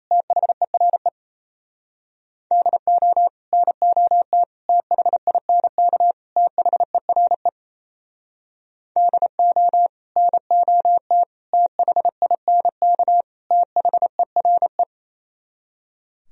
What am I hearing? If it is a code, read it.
Morse code, heard as a keyed tone, about 36 wpm, 700 Hz. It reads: THERE DO NOT THINK THERE DO NOT THINK THERE